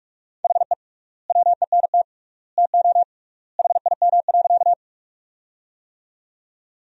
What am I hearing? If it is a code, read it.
Morse code, heard as a keyed tone, about 43 wpm, 700 Hz. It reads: HE WENT TO HIM.
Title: 3x Word-Spacing - Sentences from Top 200 Words - 45wpm